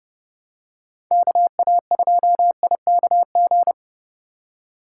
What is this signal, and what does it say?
Morse code, heard as a keyed tone, about 30 wpm, 700 Hz. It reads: KA2IKG